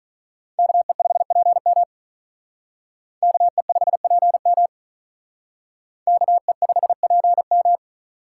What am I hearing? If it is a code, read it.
Morse code, heard as a keyed tone, about 41 wpm, 700 Hz. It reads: KE5PM KE5PM KE5PM
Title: Speed Racing - USA 2x2 Call Signs 35wpm